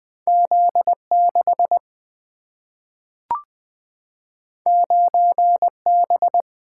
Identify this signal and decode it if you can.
Morse code, heard as a keyed tone, about 20 wpm, 700 Hz. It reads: Z 6 9 B
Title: Single-Letter Single-Number 20wpm